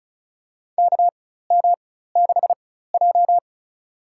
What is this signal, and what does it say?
Morse code, heard as a keyed tone, about 36 wpm, spaced as for 16 wpm, 700 Hz. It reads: KM6J
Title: USA Call Signs - 4 Characters - 4x Character Spacing - 35wpm